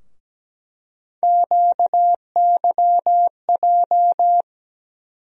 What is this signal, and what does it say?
Morse code, heard as a keyed tone, about 17 wpm, 700 Hz. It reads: QYJ
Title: USA Call Sign Postfixes - 3 Letters 17wpm